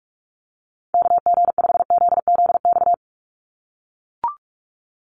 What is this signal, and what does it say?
Morse code, heard as a keyed tone, about 44 wpm, 700 Hz. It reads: KG5ZZX
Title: USA Call Signs 45wpm